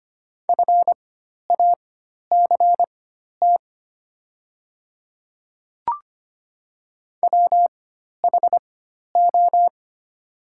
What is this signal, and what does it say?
Morse code, heard as a keyed tone, about 25 wpm, spaced as for 12 wpm, 700 Hz. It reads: FACT WHO
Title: 4x Character Spacing - Top 500 Words - 25wpm